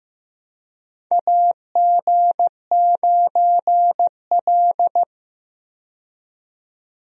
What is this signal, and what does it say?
Morse code, heard as a keyed tone, about 15 wpm, 700 Hz. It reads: AG9L
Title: USA Call Signs - 4 Characters - 15wpm